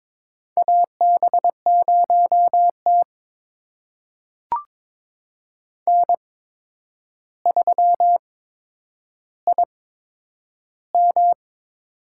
Morse code, heard as a keyed tone, about 22 wpm, spaced as for 9 wpm, 700 Hz. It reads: AB0T N3IM
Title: USA Call Signs - 4 Characters - 8x Character Spacing - 22wpm